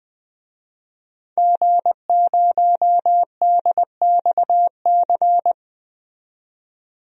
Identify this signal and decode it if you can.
Morse code, heard as a keyed tone, about 21 wpm, 700 Hz. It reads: G0DXC
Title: DXCC-100 Call Signs - 5 Characters - 20wpm